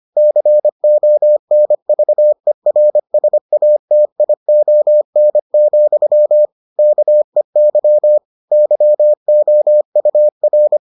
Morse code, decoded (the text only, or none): CONVERSATION, KEY YOUR